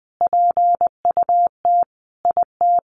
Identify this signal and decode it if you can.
Morse code, heard as a keyed tone, about 20 wpm, 700 Hz. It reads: PUT IT
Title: Sentences from Top 1100 Words 20wpm